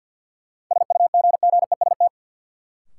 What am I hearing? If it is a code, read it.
Morse code, heard as a keyed tone, about 52 wpm, 700 Hz. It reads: SUGGEST
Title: Sentences from Top 1300 Words - New Word Drill Alt - 50wpm